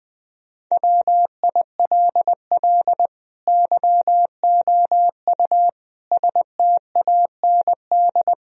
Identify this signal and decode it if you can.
Morse code, heard as a keyed tone, about 20 wpm, 700 Hz. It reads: WILL YOU STAND
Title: Sentences from Top 1000 Words - Review Part 2 - 20wpm